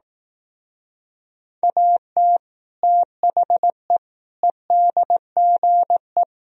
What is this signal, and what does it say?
Morse code, heard as a keyed tone, about 18 wpm, 700 Hz. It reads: AT THE EDGE
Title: Speed-Racing - Sets of 4 Words 15wpm